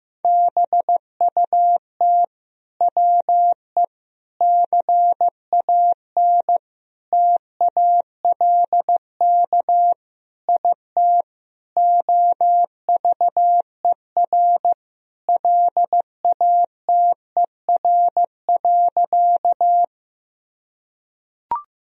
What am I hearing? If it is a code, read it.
Morse code, heard as a keyed tone, about 16 wpm, 700 Hz. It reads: BUT WE CAN TALK IT OVER LATER.
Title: Sentences from Top 1000 Words - Review Part 4 - 15wpm